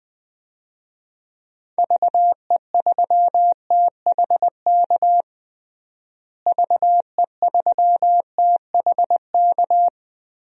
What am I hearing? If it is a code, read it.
Morse code, heard as a keyed tone, about 20 wpm, 700 Hz. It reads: VE3THK VE3THK
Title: DXCC-100 Call Signs - 6 Characters - Repeated 3x - 20wpm